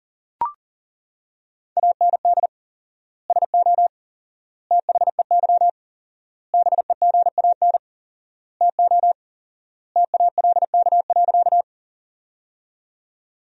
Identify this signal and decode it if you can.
Morse code, heard as a keyed tone, about 40 wpm, 700 Hz. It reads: AND SO THEY BEGAN TO TALK.
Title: 4x Word-Spacing - Sentences from Top 400 Words - 40wpm